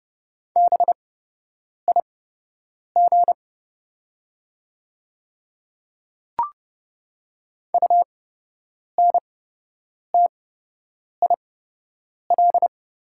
Morse code, heard as a keyed tone, about 30 wpm, spaced as for 8 wpm, 700 Hz. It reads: BIG UNTIL
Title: 8x Character Spacing - Top 500 Words - 30wpm